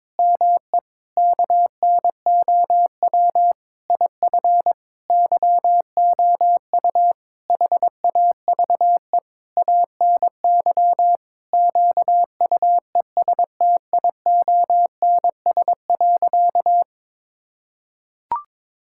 Morse code, heard as a keyed tone, about 21 wpm, 700 Hz. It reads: ME KNOW IF YOU HAVE ANY QUESTIONS.
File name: Sentences with 7-Word Phrases - 22wpm